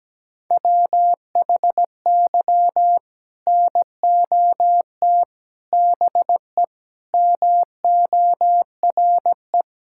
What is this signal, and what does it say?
Morse code, heard as a keyed tone, about 17 wpm, 700 Hz. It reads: WHY NOT BE MORE
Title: Sentences from Top 1600 Words 17wpm